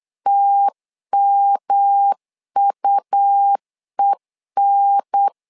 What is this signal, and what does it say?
Morse code, heard as a keyed tone, about 8 wpm, 800 Hz. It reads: TMUEN
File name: LEZIONI / 3 MWRO / 3 M 8wpm 2300